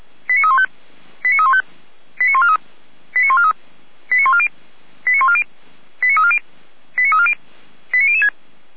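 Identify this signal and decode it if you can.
Five-tone selective call (ZVEI), heard as quick runs of five tones, tones of about 70 ms. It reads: 89315 89315 89143 89143 89140 89140 89340 89340 89006
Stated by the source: ZVEI five-tone call received by SDR.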